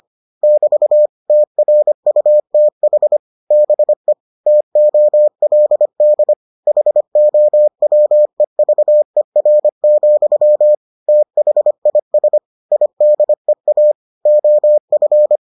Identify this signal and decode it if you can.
Morse code, heard as a keyed tone, about 25 wpm, 600 Hz. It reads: <BT> TRUTH BE TOLD HOWEVER, THIS IDEA OF